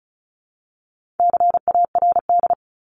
Morse code, heard as a keyed tone, about 34 wpm, 700 Hz. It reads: CARD
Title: Top 501-750 Words 35wpm